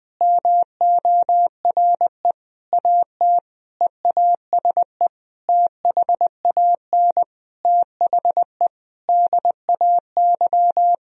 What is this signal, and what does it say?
Morse code, heard as a keyed tone, about 20 wpm, 700 Hz. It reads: MORE AT EASE THAN THE DAY
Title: Sentences from Top 1000 Words - Review Part 1 - 20wpm